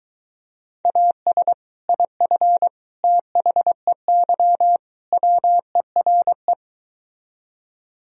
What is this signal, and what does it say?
Morse code, heard as a keyed tone, about 23 wpm, 700 Hz. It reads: AS IF THEY WERE